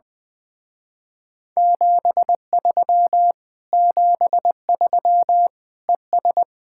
Morse code, heard as a keyed tone, about 20 wpm, 700 Hz. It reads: 73 73 ES
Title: Speed-Racing Common QSO Phrases Encore - 20wpm